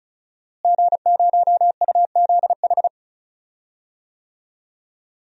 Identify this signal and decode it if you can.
Morse code, heard as a keyed tone, about 36 wpm, 700 Hz. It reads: G0UZH
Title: DXCC-100 Call Signs - 5 Characters - Repeated 3x - 35wpm